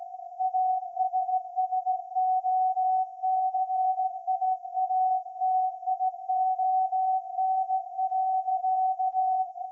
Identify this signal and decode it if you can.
Morse code, heard as a keyed tone, about 16 wpm, 750 Hz. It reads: ASSOCIATION.
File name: iz7slz / VLF / SAQ 30-06-2024 0900